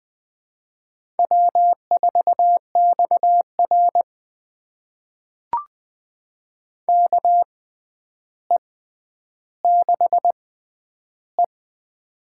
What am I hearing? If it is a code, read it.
Morse code, heard as a keyed tone, about 20 wpm, spaced as for 10 wpm, 700 Hz. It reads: W4XR KE6E
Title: USA Call Signs - 4 Characters - 6x Character Spacing - 20wpm